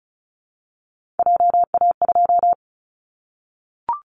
Morse code, heard as a keyed tone, about 36 wpm, 700 Hz. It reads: JA2
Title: Callsign Prefixes for Top 50 DXCC - 35wpm